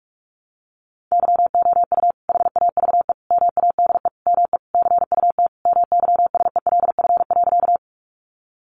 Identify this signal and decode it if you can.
Morse code, heard as a keyed tone, about 44 wpm, 700 Hz. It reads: YOU HAVE MADE ME CUT MYSELF.
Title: Sentences from Top 1300 Words 45wpm